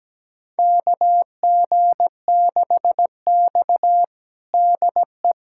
Morse code, heard as a keyed tone, about 17 wpm, 700 Hz. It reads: KG6X DE
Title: USA Callsign DE Callsign - 4 Characters - 17wpm